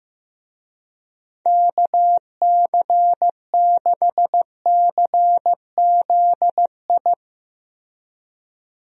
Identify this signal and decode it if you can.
Morse code, heard as a keyed tone, about 15 wpm, 700 Hz. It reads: KC6CZI
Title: USA Call Signs - 6 Characters - Repeated 2x - 15wpm